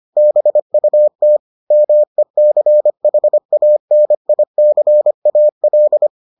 Morse code, heard as a keyed tone, about 25 wpm, 600 Hz. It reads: BUT MECHANICAL